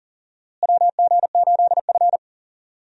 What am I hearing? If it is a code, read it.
Morse code, heard as a keyed tone, about 40 wpm, 700 Hz. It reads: WG8F
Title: USA Call Signs - 4 Characters - Repeated 2x - 40wpm